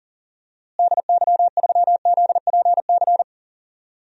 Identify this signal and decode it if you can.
Morse code, heard as a keyed tone, about 39 wpm, 700 Hz. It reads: DY3ZPC
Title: DXCC-100 Call Signs - 6 Characters - Repeated 3x - 40wpm